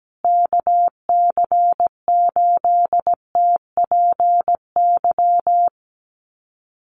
Morse code, heard as a keyed tone, about 17 wpm, 700 Hz. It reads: KC8TPY
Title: USA Call Signs Repeated 3x 17wpm